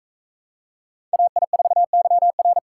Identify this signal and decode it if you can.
Morse code, heard as a keyed tone, about 42 wpm, 700 Hz. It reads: AI4YR